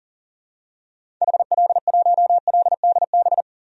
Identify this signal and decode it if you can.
Morse code, heard as a keyed tone, about 40 wpm, 700 Hz. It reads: HL1LDB
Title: DXCC-100 Call Signs - 6 Characters - 40wpm